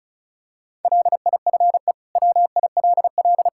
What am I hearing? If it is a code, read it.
Morse code, heard as a keyed tone, about 36 wpm, 700 Hz. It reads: LIFE WILL
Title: Sentences from Top 1000 Words - Review Part 5 - 35wpm